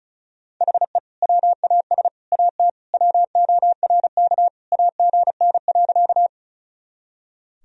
Morse code, heard as a keyed tone, about 34 wpm, 700 Hz. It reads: HE WAS AT WORK AGN.
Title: Sentences from Top 200 Words and Abbreviations 35wpm